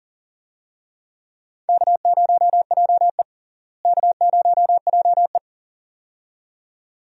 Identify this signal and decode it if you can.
Morse code, heard as a keyed tone, about 40 wpm, 700 Hz. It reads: K0JE K0JE
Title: USA Call Signs - 4 Characters - Repeated 2x - 40wpm